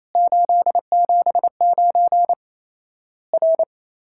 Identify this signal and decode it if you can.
Morse code, heard as a keyed tone, about 27 wpm, 700 Hz. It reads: 879 R
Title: Worked All Europe DX Contest - QTC Exchange - 28wpm